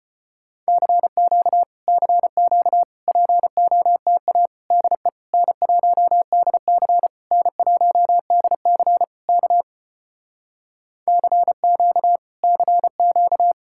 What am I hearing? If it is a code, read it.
Morse code, heard as a keyed tone, about 33 wpm, 700 Hz. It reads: CQ CQ POTA DE N1DC N1DC K CQ CQ